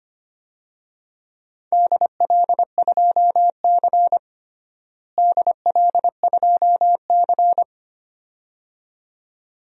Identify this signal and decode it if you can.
Morse code, heard as a keyed tone, about 25 wpm, 700 Hz. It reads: DL2C DL2C
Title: DXCC-100 Call Signs - 4 Characters - Repeated 2x - 25wpm